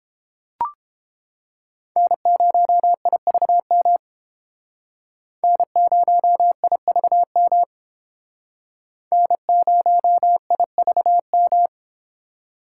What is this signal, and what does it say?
Morse code, heard as a keyed tone, about 28 wpm, 700 Hz. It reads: N0IVM N0IVM N0IVM